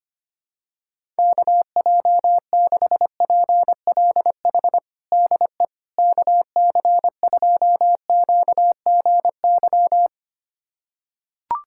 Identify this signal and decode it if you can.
Morse code, heard as a keyed tone, about 25 wpm, 700 Hz. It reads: KJ6PLH DE KC2QGY
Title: USA Callsign DE Callsign - 6 Characters - 25wpm